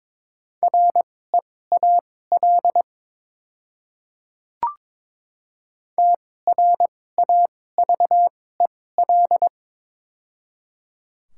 Morse code, heard as a keyed tone, about 23 wpm, spaced as for 16 wpm, 700 Hz. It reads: REAL TRAVEL
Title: 2x Character Spacing - Top 500 Words - 22wpm